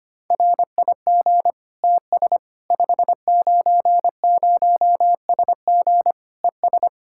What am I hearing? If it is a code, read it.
Morse code, heard as a keyed tone, about 25 wpm, 700 Hz. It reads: RIG TS 590SG ES
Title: Practice QSO - Exchange 2 - Info - 25wpm